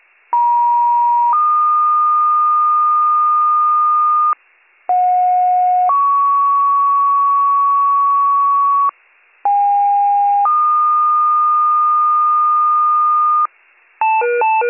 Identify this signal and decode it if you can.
Two-tone page, heard as a one-second tone, then a three-second tone, about 950 then 1220 Hz, about 730 then 1120 Hz, about 810 then 1220 Hz. It is heard from 0.3 to 13.5 s.